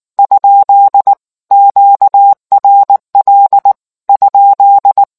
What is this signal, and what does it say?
Morse code, heard as a keyed tone, about 19 wpm, 800 Hz. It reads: ? QRL ?